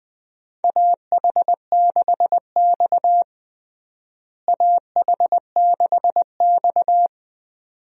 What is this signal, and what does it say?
Morse code, heard as a keyed tone, about 21 wpm, 700 Hz. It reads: AH6X AH6X
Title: USA Call Signs - 4 Characters - Repeated 2x - 20wpm